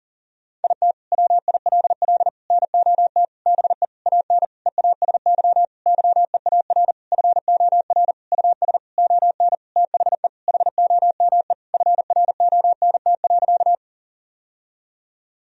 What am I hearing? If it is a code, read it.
Morse code, heard as a keyed tone, about 40 wpm, 700 Hz. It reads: IT WILL NOT BE AN EASY YEAR FOR US ON THE HOME FRONT.
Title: Sentences from Top 1000 Words - Review Part 2 - 40wpm